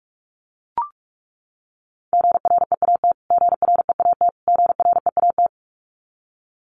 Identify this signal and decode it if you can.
Morse code, heard as a keyed tone, about 44 wpm, 700 Hz. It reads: GREAT GREAT GREAT
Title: Top 500 Words Repeated 45wpm